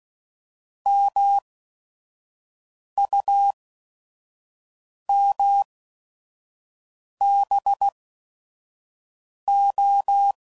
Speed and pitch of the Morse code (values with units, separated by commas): 16 wpm, spaced as for 5 wpm, 800 Hz